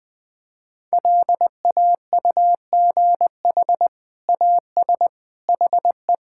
Morse code, heard as a keyed tone, about 20 wpm, 700 Hz. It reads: LAUGH AS HE
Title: Sentences with 2-Word Phrases - Spoken Beginning - 20wpm